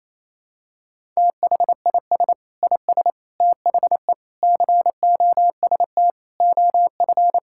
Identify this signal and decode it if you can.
Morse code, heard as a keyed tone, about 28 wpm, 700 Hz. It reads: THIS IS THE COST OF
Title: Sentences from Top 1000 Words - Review Part 5 - 28wpm